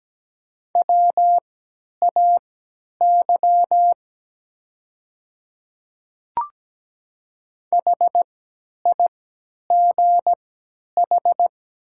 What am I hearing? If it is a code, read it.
Morse code, heard as a keyed tone, about 17 wpm, spaced as for 10 wpm, 700 Hz. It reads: WAY HIGH
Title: 3x Character Spacing - Top 200 Words - 17wpm